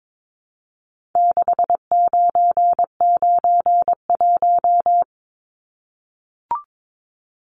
Morse code, heard as a keyed tone, about 22 wpm, 700 Hz. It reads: 6991
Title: Numbers - 4-Digit 22wpm